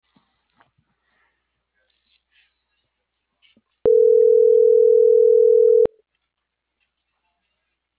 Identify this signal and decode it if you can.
Ringback tone (North America), 440 + 480 Hz.